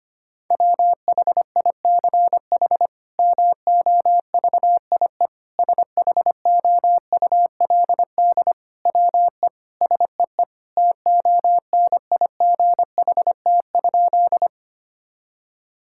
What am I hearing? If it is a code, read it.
Morse code, heard as a keyed tone, about 25 wpm, 700 Hz. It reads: WHICH MOVIE SHOULD WE SEE TONIGHT?
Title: Sentences from Top 1000 Words - Review Part 5 - 25wpm